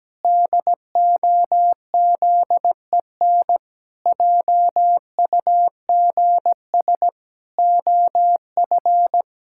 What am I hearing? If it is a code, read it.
Morse code, heard as a keyed tone, about 17 wpm, 700 Hz. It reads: DOZEN JUGS OF
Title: Pangram Challenge v2 - 17wpm